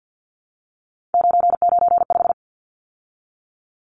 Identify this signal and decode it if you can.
Morse code, heard as a keyed tone, about 50 wpm, 700 Hz. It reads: 995
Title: Numbers - 3-Digit 50wpm